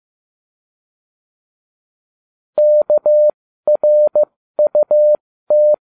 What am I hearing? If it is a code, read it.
Morse code, heard as a keyed tone, about 15 wpm, spaced as for 13 wpm, 600 Hz. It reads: KRUT